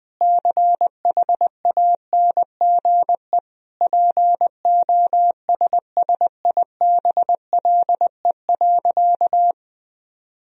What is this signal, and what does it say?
Morse code, heard as a keyed tone, about 21 wpm, 700 Hz. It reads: CHANGE POSSIBLE.
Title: Sentences from Top 1000 Words - Review Part 1 - 20wpm